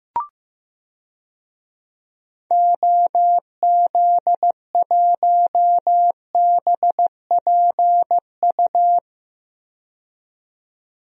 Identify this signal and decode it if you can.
Morse code, heard as a keyed tone, about 15 wpm, 700 Hz. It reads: OZ1BPU
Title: DXCC-100 Call Signs - 6 Characters - 15wpm